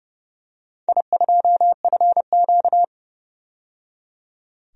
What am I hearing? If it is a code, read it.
Morse code, heard as a keyed tone, about 29 wpm, 700 Hz. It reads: I2FQ
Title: DXCC-100 Call Signs - 4 Characters - 30wpm